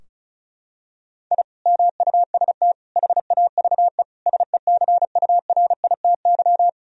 Morse code, heard as a keyed tone, about 36 wpm, 700 Hz. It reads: I MUST HAVE SECURITY
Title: Sentences from Top 1000 Words - Review Part 4 - 35wpm